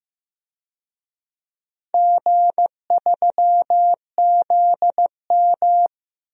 Morse code, heard as a keyed tone, about 15 wpm, 700 Hz. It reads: G3ZM
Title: DXCC-100 Call Signs - 4 Characters - Repeated 3x - 15wpm